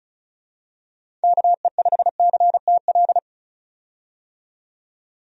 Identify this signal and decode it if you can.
Morse code, heard as a keyed tone, about 34 wpm, 700 Hz. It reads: KE5CTL
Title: USA Call Signs - 6 Characters - 35wpm